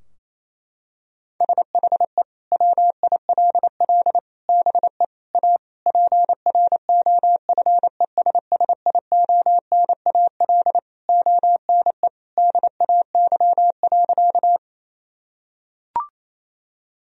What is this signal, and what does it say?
Morse code, heard as a keyed tone, about 27 wpm, 700 Hz. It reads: SHE WILL BE A PROFESSIONAL ONE DAY.
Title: Sentences from Top 1000 Words - Review Part 4 - 28wpm